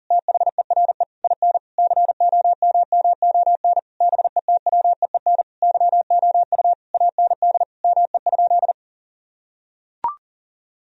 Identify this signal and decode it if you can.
Morse code, heard as a keyed tone, about 37 wpm, 700 Hz. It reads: THERE IN COMMON BETWEEN YOU AND ME?
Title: Sentences from Top 1000 Words - Review Part 3 - 40wpm